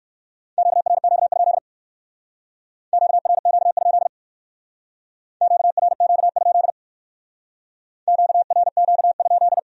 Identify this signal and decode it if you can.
Morse code, heard as a keyed tone, about 61 wpm, 700 Hz. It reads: QRQ? QRQ? QRQ? QRQ?